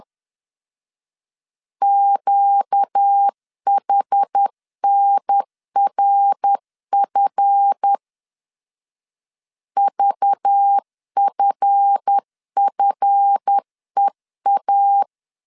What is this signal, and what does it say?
Morse code, heard as a keyed tone, about 10 wpm, 800 Hz. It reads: QHNRF VFFEA
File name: LEZIONI / 7 GF9Z / 7 GF 10wpm 2514